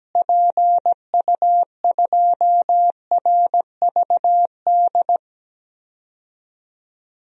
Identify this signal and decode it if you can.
Morse code, heard as a keyed tone, about 17 wpm, 700 Hz. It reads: PU2RVD
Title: DXCC-100 Call Signs - 6 Characters - Repeated 2x - 17wpm